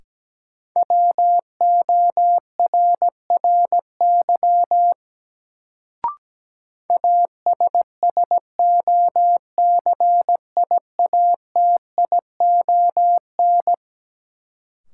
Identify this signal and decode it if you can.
Morse code, heard as a keyed tone, about 17 wpm, 700 Hz. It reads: WORRY ASSOCIATION
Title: Sentences from Top 1100 Words - New Word Drill Alt - 17wpm